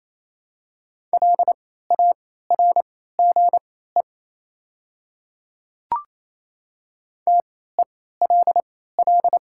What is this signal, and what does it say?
Morse code, heard as a keyed tone, about 27 wpm, spaced as for 16 wpm, 700 Hz. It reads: LARGE TELL
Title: 3x Character Spacing - Top 300 Words - 28wpm